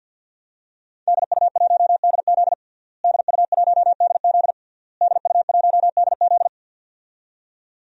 Morse code, heard as a keyed tone, about 52 wpm, 700 Hz. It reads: DU1DZ DU1DZ DU1DZ
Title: DXCC-100 Call Signs - 5 Characters - Repeated 3x - 50wpm